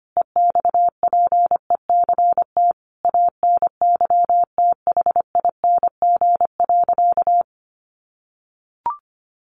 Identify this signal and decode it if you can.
Morse code, heard as a keyed tone, about 25 wpm, 700 Hz. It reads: EXPECT ANYTHING.
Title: Sentences from Top 1400 Words 25wpm